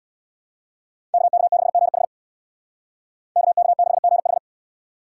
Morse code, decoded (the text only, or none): KK6GH KK6GH